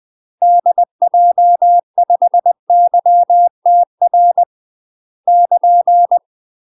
Morse code, heard as a keyed tone, about 20 wpm, 700 Hz. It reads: DJ5YTR <KN>
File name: UK MASTER QSOs TRAINING/20WPM